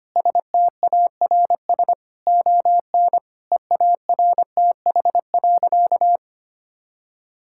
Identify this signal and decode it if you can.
Morse code, heard as a keyed tone, about 25 wpm, 700 Hz. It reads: STARS ON EARTH.
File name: practice / 25wpm / Difficult Sentences with Hints - 25wpm